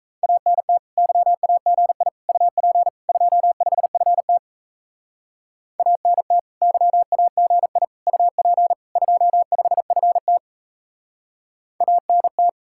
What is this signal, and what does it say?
Morse code, heard as a keyed tone, about 35 wpm, 700 Hz. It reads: ANT YAGI UP 25FT ANT YAGI UP 25FT ANT
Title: Speed-Racing Common QSO Phrases Encore - 28wpm